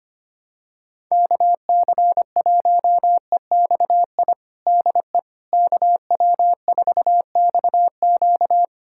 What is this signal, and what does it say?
Morse code, heard as a keyed tone, about 25 wpm, 700 Hz. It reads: KC1EXI DE KW4XQ
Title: USA Callsign DE Callsign 25wpm